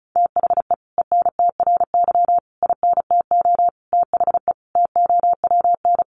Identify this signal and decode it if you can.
Morse code, heard as a keyed tone, about 34 wpm, 700 Hz. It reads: THE ENTRY INTO THE TOWN
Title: Sentences from Top 1500 Words 35wpm